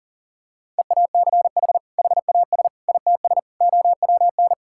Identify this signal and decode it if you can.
Morse code, heard as a keyed tone, about 37 wpm, 700 Hz. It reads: EACH HAS ITS OWN
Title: Sentences from Top 1000 Words - Review Part 5 - 40wpm